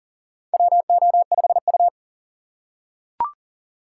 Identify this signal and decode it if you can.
Morse code, heard as a keyed tone, about 41 wpm, 690 Hz. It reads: WO5U
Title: USA Call Signs - 4 Characters - 2x Character Spacing - 40wpm